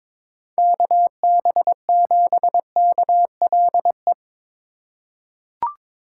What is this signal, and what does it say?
Morse code, heard as a keyed tone, about 22 wpm, 700 Hz. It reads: KB7KLE
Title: USA Call Signs - 6 Characters - 2x Character Spacing - 22wpm